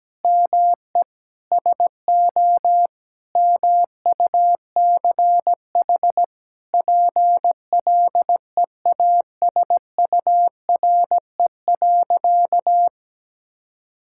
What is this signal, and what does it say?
Morse code, heard as a keyed tone, about 17 wpm, 700 Hz. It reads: ME SO MUCH PLEASURE.
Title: Sentences from Top 1600 Words 17wpm